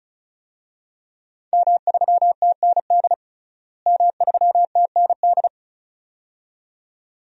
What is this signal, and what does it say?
Morse code, heard as a keyed tone, about 36 wpm, 700 Hz. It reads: M3TND M3TND
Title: DXCC-100 Call Signs - 5 Characters - Repeated 2x - 35wpm